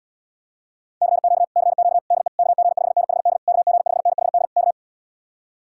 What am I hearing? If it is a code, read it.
Morse code, heard as a keyed tone, about 79 wpm, 700 Hz. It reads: CQ CQ DE KG5NHD KG5NHD K